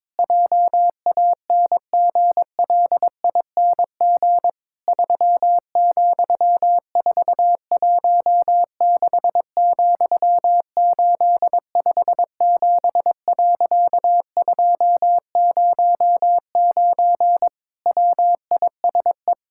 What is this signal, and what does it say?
Morse code, heard as a keyed tone, about 22 wpm, 700 Hz. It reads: JANGLING 3,416,857.209 WISE
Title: Pangram Challenge v2 - 22wpm